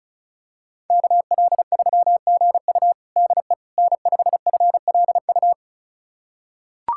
Morse code, heard as a keyed tone, about 36 wpm, 700 Hz. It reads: KL3GU DE N5FLU
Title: USA Callsign DE Callsign - 5 Characters - 35wpm